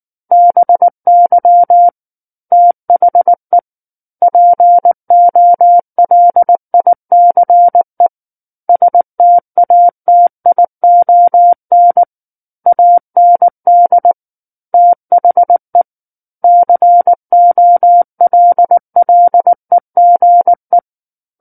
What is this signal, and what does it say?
Morse code, heard as a keyed tone, about 19 wpm, 700 Hz. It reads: BY THE POLICE STATION AND THE COLLEGE
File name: Book One / 10-Chapter 10 - 19 WPM